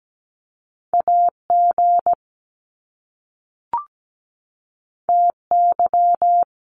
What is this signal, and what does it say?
Morse code, heard as a keyed tone, about 17 wpm, 700 Hz. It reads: AG TY
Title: Top Two Letter Combinations in Words 17wpm